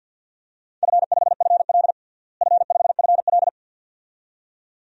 Morse code, heard as a keyed tone, about 47 wpm, 700 Hz. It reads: F5FL F5FL
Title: DXCC-100 Call Signs - 4 Characters - Repeated 2x - 50wpm